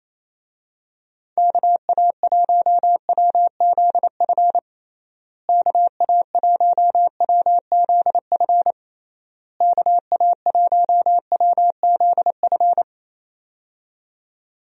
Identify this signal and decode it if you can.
Morse code, heard as a keyed tone, about 28 wpm, 700 Hz. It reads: KA1WZF KA1WZF KA1WZF
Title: USA Call Signs - 6 Characters - Repeated 3x - 28wpm